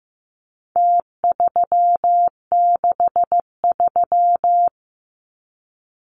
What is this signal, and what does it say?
Morse code, heard as a keyed tone, about 15 wpm, 700 Hz. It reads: T363
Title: Numbers - Contest Style - 4-Digit 15wpm